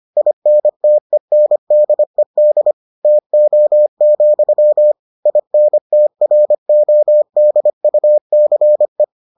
Morse code, heard as a keyed tone, about 25 wpm, 600 Hz. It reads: INTENDED TO, INTRODUCE